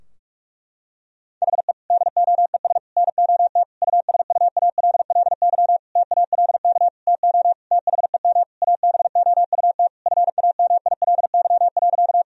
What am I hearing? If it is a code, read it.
Morse code, heard as a keyed tone, about 46 wpm, 700 Hz. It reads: HE DOES NOT USUALLY TALK TO THEM ABOUT FAMILY.
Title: Sentences from Top 1000 Words - Review Part 3 - 45wpm